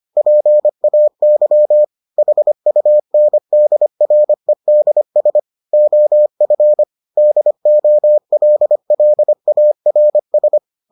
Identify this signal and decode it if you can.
Morse code, heard as a keyed tone, about 25 wpm, 600 Hz. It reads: PAY HUNDREDS OF DOLLARS